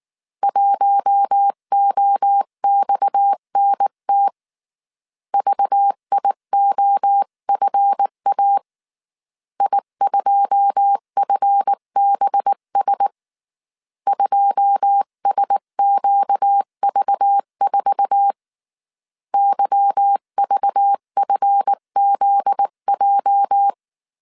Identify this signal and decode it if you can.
Morse code, heard as a keyed tone, about 19 wpm, 800 Hz. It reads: 1OXNT VIOFA I2FBS 2SQV4 YVFZJ
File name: CW / DET / 1 9 1500C S C 18wpm 2539